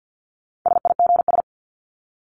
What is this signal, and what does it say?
Morse code, heard as a keyed tone, about 50 wpm, 700 Hz. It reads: SIGS
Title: Abbreviations 50wpm